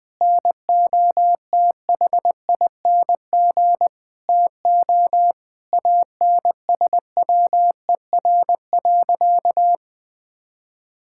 Morse code, heard as a keyed tone, about 20 wpm, 700 Hz. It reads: NOTHING TO ANSWER.
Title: Sentences from Top 1600 Words 20wpm